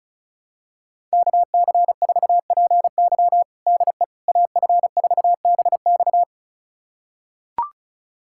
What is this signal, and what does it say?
Morse code, heard as a keyed tone, about 36 wpm, 700 Hz. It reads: KC4PY DE AF4BX
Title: USA Callsign DE Callsign - 5 Characters - 35wpm